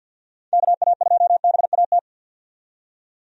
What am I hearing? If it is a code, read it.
Morse code, heard as a keyed tone, about 52 wpm, 700 Hz. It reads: KA2BAT
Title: USA Call Signs - 6 Characters - 50wpm